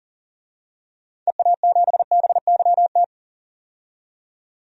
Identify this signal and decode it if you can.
Morse code, heard as a keyed tone, about 40 wpm, 700 Hz. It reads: EA7BYT
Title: DXCC-100 Call Signs - 6 Characters - 40wpm